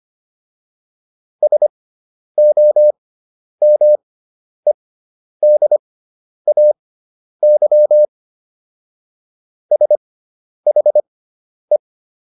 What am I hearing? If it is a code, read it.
Morse code, heard as a keyed tone, about 25 wpm, spaced as for 10 wpm, 600 Hz. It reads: SOMEDAY SHE